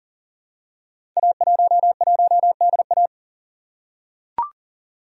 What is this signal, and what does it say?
Morse code, heard as a keyed tone, about 38 wpm, 690 Hz. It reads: A11DA
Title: Code Groups - 5 Letter:Numbers - 40wpm